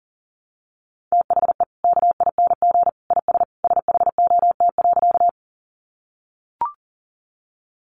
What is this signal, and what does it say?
Morse code, heard as a keyed tone, about 40 wpm, 700 Hz. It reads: THE KING IS SHOT.
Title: Sentences from Top 1100 Words 40wpm